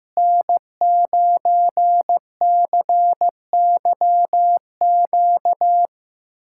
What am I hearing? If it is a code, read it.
Morse code, heard as a keyed tone, about 15 wpm, 700 Hz. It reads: N9CYQ
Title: USA Callsign DE Callsign - 5 Characters - 15wpm